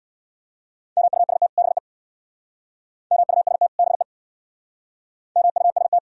Morse code, heard as a keyed tone, about 62 wpm, 700 Hz. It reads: MUST BE MUST BE MUST
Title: Speed-Racing - Sets of 2 Words 50wpm